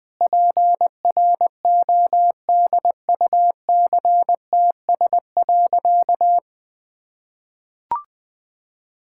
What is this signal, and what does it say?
Morse code, heard as a keyed tone, about 20 wpm, 700 Hz. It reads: PRODUCTS.
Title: Sentences with 6-Word Phrases - 20wpm